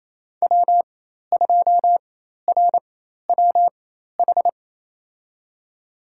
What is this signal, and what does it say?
Morse code, heard as a keyed tone, about 27 wpm, spaced as for 13 wpm, 700 Hz. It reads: W2RWH
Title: USA Call Signs - 5 Characters - 4x Character Spacing - 28wpm